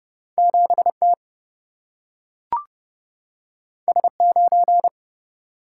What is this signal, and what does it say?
Morse code, heard as a keyed tone, about 30 wpm, 700 Hz. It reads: 7T S9
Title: Single-Letter Single-Number 30wpm